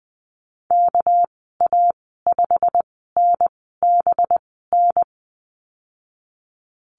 Morse code, heard as a keyed tone, about 20 wpm, spaced as for 14 wpm, 700 Hz. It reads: KA5NBN
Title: 200 Lesson 32g - Letter k - Call Signs 20wpm